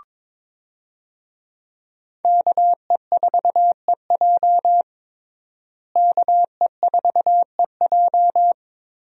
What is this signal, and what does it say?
Morse code, heard as a keyed tone, about 22 wpm, 700 Hz. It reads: KE4EJ KE4EJ
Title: USA Call Signs - 5 Characters - Repeated 3x - 22wpm